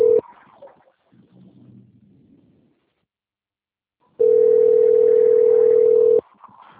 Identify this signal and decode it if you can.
Ringback tone (North America), 440 + 480 Hz.